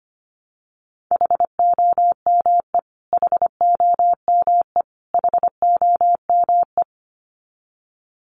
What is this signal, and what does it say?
Morse code, heard as a keyed tone, about 25 wpm, 700 Hz. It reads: HOME HOME HOME
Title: Top 200 Words Repeated 25wpm